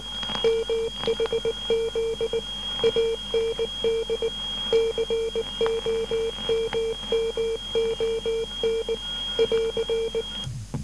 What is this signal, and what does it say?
Morse code, heard as a keyed tone, about 19 wpm, 450 Hz, with noise. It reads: MHZ AND COMMON <AR>
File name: SAG6 12880.5 1218